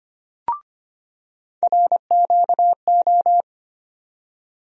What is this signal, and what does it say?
Morse code, heard as a keyed tone, about 25 wpm, 700 Hz. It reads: RQO
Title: USA Call Sign Postfixes - 3 Letters 25wpm